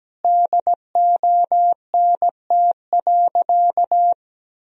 Morse code, heard as a keyed tone, about 17 wpm, 700 Hz. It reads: DONT.
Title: Sentences from Top 300 Words and Abbreviations 17wpm